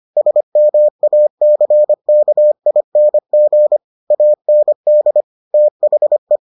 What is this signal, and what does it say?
Morse code, heard as a keyed tone, about 25 wpm, 600 Hz. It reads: SMACKING AND THE